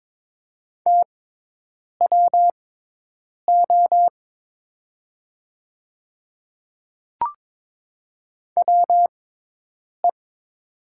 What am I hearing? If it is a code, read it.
Morse code, heard as a keyed tone, about 22 wpm, spaced as for 8 wpm, 700 Hz. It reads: TWO WE
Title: 6x Character Spacing - Top 200 Words - 22wpm